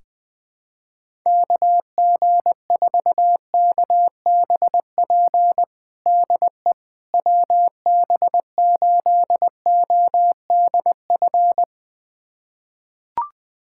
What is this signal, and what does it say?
Morse code, heard as a keyed tone, about 21 wpm, 700 Hz. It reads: KG4KBP DE WB8ODF
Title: USA Callsign DE Callsign 20wpm